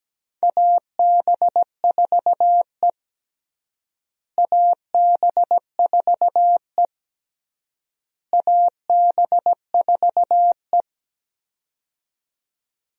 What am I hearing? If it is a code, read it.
Morse code, heard as a keyed tone, about 17 wpm, 700 Hz. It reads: AB4E AB4E AB4E
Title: USA Call Signs - 4 Characters - Repeated 3x - 17wpm